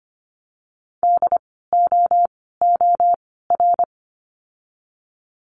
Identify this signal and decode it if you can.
Morse code, heard as a keyed tone, about 25 wpm, 700 Hz. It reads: D O O R
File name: course / 25:10wpm / 094 Lesson 16f - Letter c - Words 25wpm